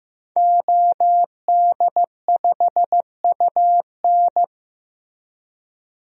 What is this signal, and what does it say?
Morse code, heard as a keyed tone, about 15 wpm, 700 Hz. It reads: OD5UN
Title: DXCC-100 Call Signs - 5 Characters - Repeated 3x - 15wpm